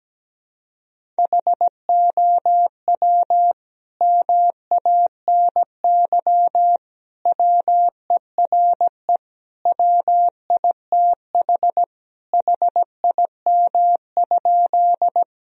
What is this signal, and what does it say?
Morse code, heard as a keyed tone, about 17 wpm, 700 Hz. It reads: HOW MANY WERE WITH HIM?
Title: Sentences from Top 1000 Words - Review Part 1 - 17wpm